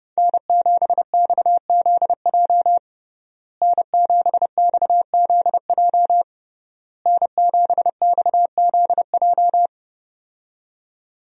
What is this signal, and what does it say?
Morse code, heard as a keyed tone, about 32 wpm, 700 Hz. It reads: N7XZJ N7XZJ N7XZJ
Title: USA Call Signs - 5 Characters - Repeated 3x - 30wpm